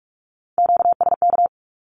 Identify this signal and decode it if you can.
Morse code, heard as a keyed tone, about 44 wpm, 700 Hz. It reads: QSK
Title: Q-Codes Repeated 3x 45wpm